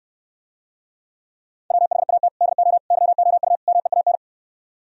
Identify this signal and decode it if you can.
Morse code, heard as a keyed tone, about 68 wpm, 700 Hz. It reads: WHAT DO YOU MEAN